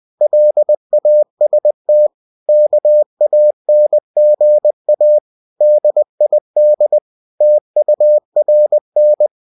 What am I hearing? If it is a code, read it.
Morse code, heard as a keyed tone, about 20 wpm, 600 Hz. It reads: LAST KANGA DID TURN
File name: cw-books / winnie-the-pooh / winnie-20wpm-page-0069